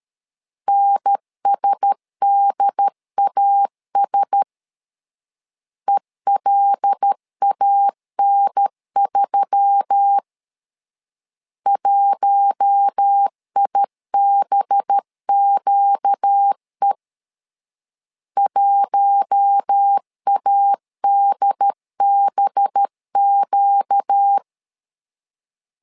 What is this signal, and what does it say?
Morse code, heard as a keyed tone, about 13 wpm, 800 Hz. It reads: NSDAS ELAN3 1IBQE 1ADBQ
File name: LEZIONI / 5 LBVC / 5 LB 12wpm 1906